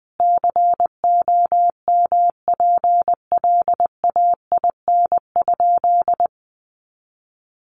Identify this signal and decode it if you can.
Morse code, heard as a keyed tone, about 20 wpm, 700 Hz. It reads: COMPLAIN?
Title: Sentences from Top 1300 Words 20wpm